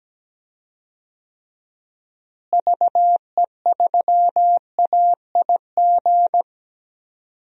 Morse code, heard as a keyed tone, about 17 wpm, 700 Hz. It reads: VE3AIG
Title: DXCC-100 Call Signs - 6 Characters - Repeated 3x - 17wpm